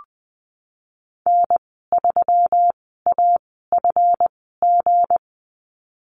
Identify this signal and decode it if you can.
Morse code, heard as a keyed tone, about 20 wpm, spaced as for 15 wpm, 700 Hz. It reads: N3AFG
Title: 179 Lesson 29g - Slash - Call Signs 20wpm